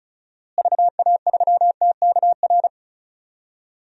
Morse code, heard as a keyed tone, about 36 wpm, 700 Hz. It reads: VA3TKR
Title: DXCC-100 Call Signs - 6 Characters - Repeated 2x - 35wpm